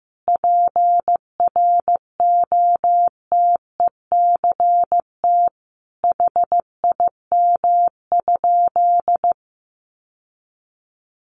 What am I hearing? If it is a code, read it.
Morse code, heard as a keyed tone, about 15 wpm, 700 Hz. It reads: PROTECT HIM?
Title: Sentences from Top 900 Words 15wpm